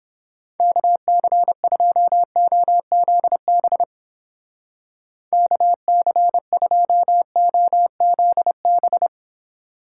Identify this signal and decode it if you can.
Morse code, heard as a keyed tone, about 29 wpm, 700 Hz. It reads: KC2OZB KC2OZB